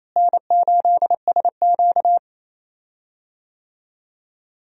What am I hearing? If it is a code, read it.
Morse code, heard as a keyed tone, about 28 wpm, 700 Hz. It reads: N8SQ
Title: USA Call Signs - 4 Characters - Repeated 2x - 28wpm